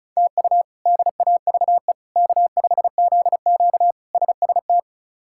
Morse code, heard as a keyed tone, about 36 wpm, 700 Hz. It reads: TU DAVE K5ZQ SST
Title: K1USN SST QSO Practice - 35wpm